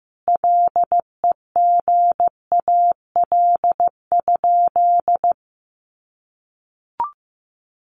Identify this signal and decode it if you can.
Morse code, heard as a keyed tone, about 15 wpm, 700 Hz. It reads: LEGAL?
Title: Sentences from Top 1200 Words 15wpm